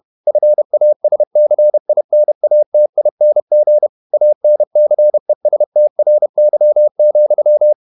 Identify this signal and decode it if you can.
Morse code, heard as a keyed tone, about 31 wpm, 600 Hz. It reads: FASCINATING ANCESTRY,
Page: Never 31wpm | CW med Gnister